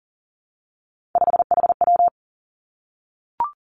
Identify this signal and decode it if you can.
Morse code, heard as a keyed tone, about 40 wpm, 700 Hz. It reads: 5HW
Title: Three Challenging Letter-Number Combinations 40wpm